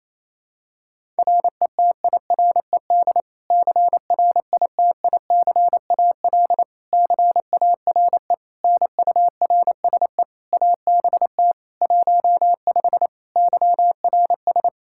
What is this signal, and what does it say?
Morse code, heard as a keyed tone, about 28 wpm, 700 Hz. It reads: RETIRED CRITICAL CARE NURSE ABT 15 YRS
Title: Practice QSO - Exchange 3 - More Info - 28wpm